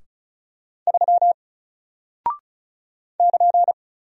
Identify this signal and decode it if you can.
Morse code, heard as a keyed tone, about 33 wpm, 700 Hz. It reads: 3 <KN>
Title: Single Character (Letter:Number:Punctuation:Prosign) Alt 35wpm